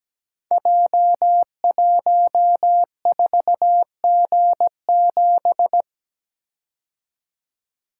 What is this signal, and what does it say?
Morse code, heard as a keyed tone, about 17 wpm, 700 Hz. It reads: J14G7
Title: Code Groups - 5 Letter:Numbers - 17wpm